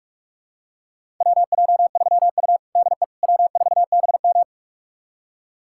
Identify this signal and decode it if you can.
Morse code, heard as a keyed tone, about 46 wpm, 700 Hz. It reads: WJ3U DE W4BM
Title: USA Callsign DE Callsign - 4 Characters - 45wpm